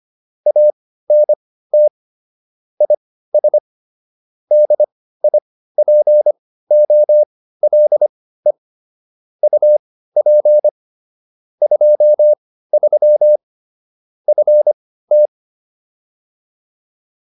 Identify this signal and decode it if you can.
Morse code, heard as a keyed tone, about 25 wpm, spaced as for 15 wpm, 600 Hz. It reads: ANT IS DIPOLE UP 23 FT
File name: QSO308 15